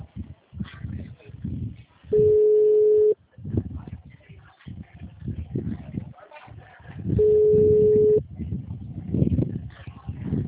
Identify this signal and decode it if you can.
Ringback tone (Europe, 425 Hz).